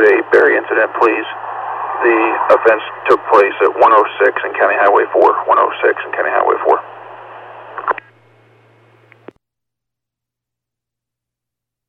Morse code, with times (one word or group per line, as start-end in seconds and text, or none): none